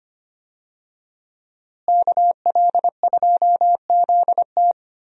Morse code, heard as a keyed tone, about 25 wpm, 700 Hz. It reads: KL2ZT
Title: USA Call Signs - 5 Characters - Repeated 2x - 25wpm